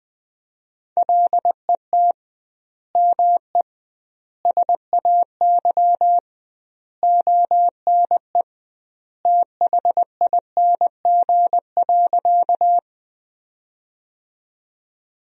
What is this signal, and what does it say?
Morse code, heard as a keyed tone, about 20 wpm, 700 Hz. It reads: LET ME SAY ONE THING.
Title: 2x Word-Spacing - Sentences from Top 300 Words - 20wpm